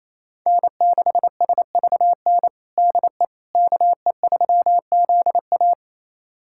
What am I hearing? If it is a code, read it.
Morse code, heard as a keyed tone, about 28 wpm, 700 Hz. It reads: N6SVN DE KE3ZA
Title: USA Callsign DE Callsign - 5 Characters - 28wpm